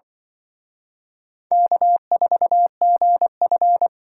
Morse code, heard as a keyed tone, about 24 wpm, 700 Hz. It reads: K4GF
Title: Speed Racing - USA 1x2 Call Signs 20wpm